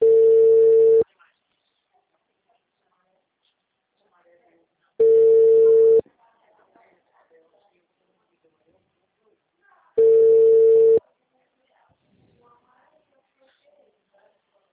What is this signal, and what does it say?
Ringback tone (China, 450 Hz), with speech.
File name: out- -2005- - - .24504